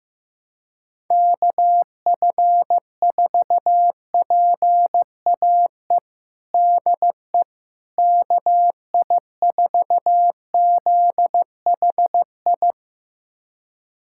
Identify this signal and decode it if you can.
Morse code, heard as a keyed tone, about 15 wpm, 700 Hz. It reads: KF4PAE DE KI4ZHI
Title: USA Callsign DE Callsign - 6 Characters - 15wpm